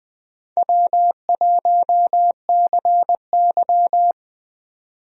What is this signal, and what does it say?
Morse code, heard as a keyed tone, about 20 wpm, 700 Hz. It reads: W1CY
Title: USA Call Signs - 4 Characters - 20wpm